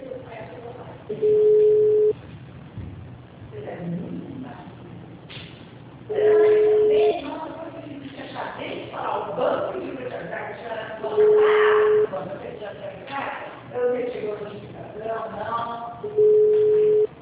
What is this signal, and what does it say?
Ringback tone (Europe, 425 Hz), with speech.